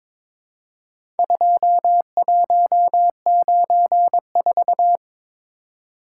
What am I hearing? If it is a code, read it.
Morse code, heard as a keyed tone, about 22 wpm, 700 Hz. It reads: 2194
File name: practice / 22wpm / Four Letter-Number Combinations - 22wpm